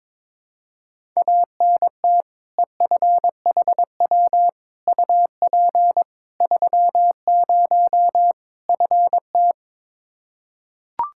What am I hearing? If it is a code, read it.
Morse code, heard as a keyed tone, about 22 wpm, 700 Hz. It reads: ANT EFHW UP 30 FT
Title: Practice QSO - Exchange 2 - Info - 22wpm